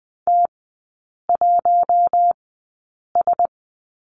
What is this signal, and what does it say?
Morse code, heard as a keyed tone, about 20 wpm, spaced as for 8 wpm, 700 Hz. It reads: T1S
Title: 041 Lesson 8e - Number 1 - Three Characters 20wpm